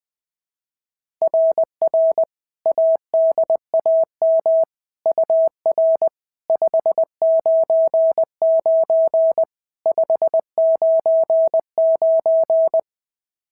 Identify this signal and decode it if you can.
Morse code, heard as a keyed tone, about 18 wpm, 650 Hz. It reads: RR ADAM UR 599 599
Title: SOTA QSO Practice 20wpm